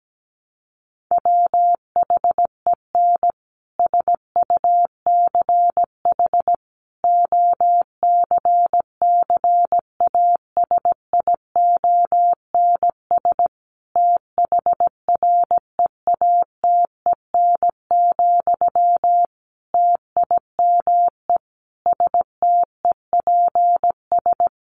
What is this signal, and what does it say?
Morse code, heard as a keyed tone, about 17 wpm, 700 Hz. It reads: WHEN SUCH OCCASIONS THREATEN, TIME STEPS
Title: A Sound Of Thunder 17wpm